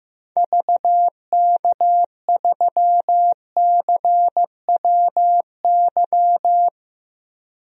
Morse code, heard as a keyed tone, about 15 wpm, 700 Hz. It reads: VK3CWY
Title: DXCC-100 Call Signs - 6 Characters - Repeated 2x - 15wpm